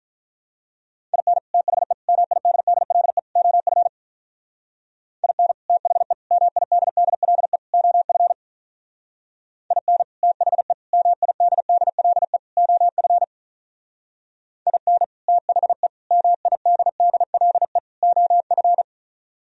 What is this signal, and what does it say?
Morse code, heard as a keyed tone, about 45 wpm, 700 Hz. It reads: IN THE MIDDLE OF IN THE MIDDLE OF IN THE MIDDLE OF IN THE MIDDLE OF